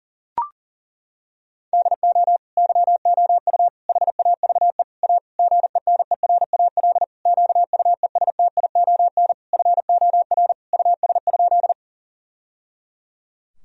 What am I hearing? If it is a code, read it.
Morse code, heard as a keyed tone, about 37 wpm, 700 Hz. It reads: DO YOU HAVE A GENERAL QUESTION FOR US?
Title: Sentences from Top 1000 Words - Review Part 3 - 40wpm